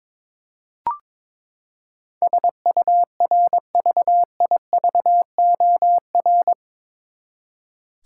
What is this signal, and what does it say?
Morse code, heard as a keyed tone, about 22 wpm, 700 Hz. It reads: SURVIVOR
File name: practice / 22wpm / Eight Letter Words - 22wpm